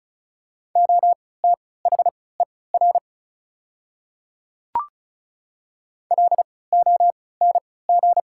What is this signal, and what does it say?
Morse code, heard as a keyed tone, about 34 wpm, spaced as for 20 wpm, 690 Hz. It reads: OTHER LONG
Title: 3x Character Spacing - Top 500 Words - 35wpm